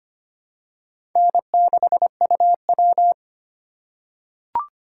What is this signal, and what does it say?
Morse code, heard as a keyed tone, about 25 wpm, 700 Hz. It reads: N6UW
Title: USA Call Signs - 4 Characters - 2x Character Spacing - 25wpm